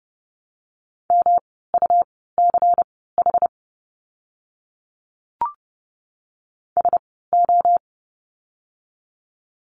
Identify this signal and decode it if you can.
Morse code, heard as a keyed tone, about 30 wpm, spaced as for 17 wpm, 700 Hz. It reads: MUCH SO
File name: course / 30:10wpm / 230 Lesson 37f - Letter x - Words 30wpm